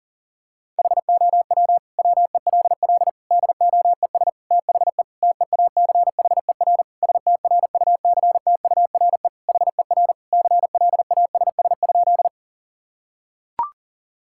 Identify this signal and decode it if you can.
Morse code, heard as a keyed tone, about 40 wpm, 700 Hz. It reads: HOW WELL DOES THE TEACHER STRUCTURE HER CLASS?
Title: Sentences from Top 1000 Words - Review Part 6 - 40wpm